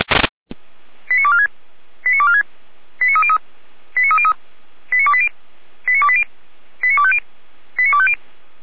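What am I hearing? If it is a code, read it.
Five-tone selective call (ZVEI), heard as quick runs of five tones, tones of about 70 ms. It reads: 89246 89246 89303 89303 89280 89280 89250 89250
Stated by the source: ZVEI five-tone alert received by SDR, feuerwehr-orlamuende.org fire brigade audio server.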